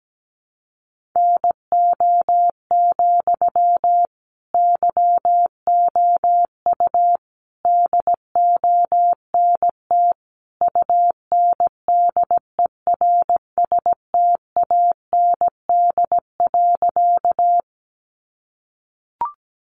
Morse code, heard as a keyed tone, about 17 wpm, 700 Hz. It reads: NO, YOU DONT UNDERSTAND.
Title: Sentences from Top 600 Words 17wpm